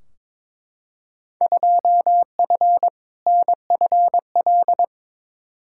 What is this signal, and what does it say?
Morse code, heard as a keyed tone, about 22 wpm, 700 Hz. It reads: 2F NFL
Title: Field Day Exchanges - 22wpm